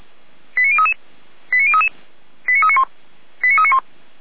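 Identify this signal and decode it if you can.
Five-tone selective call (ZVEI), heard as quick runs of five tones, tones of about 80 ms. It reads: 89033 89033 89391 89391
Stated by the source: ZVEI five-tone selective call received on an SDR by the fire brigade Feuerwehr Orlamünde.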